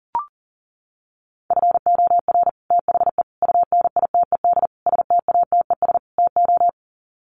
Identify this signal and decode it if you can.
Morse code, heard as a keyed tone, about 37 wpm, 700 Hz. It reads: FOR THE UNITED STATES TO
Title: Sets of 5 Words Encore 40wpm